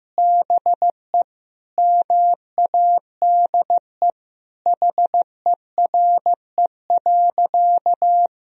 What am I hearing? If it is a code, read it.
Morse code, heard as a keyed tone, about 16 wpm, 700 Hz. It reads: BE MADE HERE.
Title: Sentences from Top 1000 Words - Review Part 4 - 15wpm